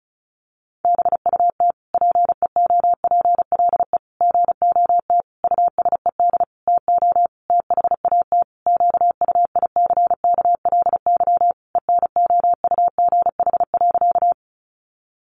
Morse code, heard as a keyed tone, about 34 wpm, 700 Hz. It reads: BUT PEOPLE GOT USED TO THAT QUICKLY ENOUGH.
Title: Sentences from Top 700 Words 35wpm